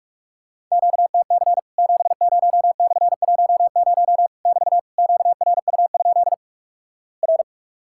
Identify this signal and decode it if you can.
Morse code, heard as a keyed tone, about 46 wpm, 700 Hz. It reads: QTC 70/10 <BT> QRU? R
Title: Worked All Europe DX Contest - QTC Exchange - 45wpm